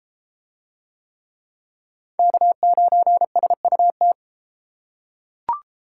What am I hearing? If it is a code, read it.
Morse code, heard as a keyed tone, about 33 wpm, 700 Hz. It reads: K9SUT